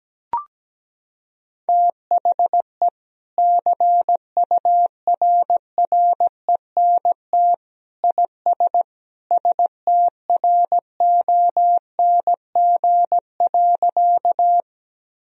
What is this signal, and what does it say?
Morse code, heard as a keyed tone, about 17 wpm, 700 Hz. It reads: THE CURRENT IS STRONG.
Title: Sentences from Top 1000 Words - Review Part 4 - 17wpm